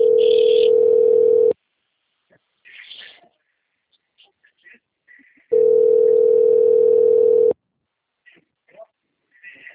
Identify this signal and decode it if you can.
Ringback tone (North America), 440 + 480 Hz.